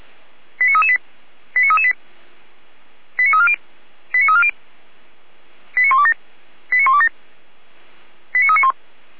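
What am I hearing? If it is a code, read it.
Five-tone selective call (ZVEI), heard as quick runs of five tones, tones of about 70 ms. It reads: 89338 89338 89340 89340 89127 89127 89391
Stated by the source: ZVEI five-tone call 89315 received on an SDR by feuerwehr-orlamuende.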